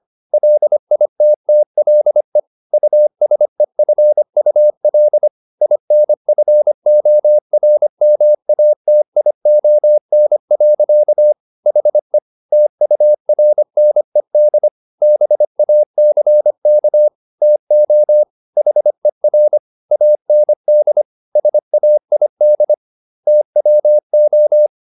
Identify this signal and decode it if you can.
Morse code, heard as a keyed tone, about 25 wpm, 600 Hz. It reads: LITTLE USEFUL INFORMATION. HE TURNED BACK TO HER AND SAID TWO